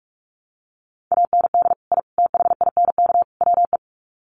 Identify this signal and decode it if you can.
Morse code, heard as a keyed tone, about 44 wpm, 700 Hz. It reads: AND I THINK WE
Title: Sets of 4 Words 45wpm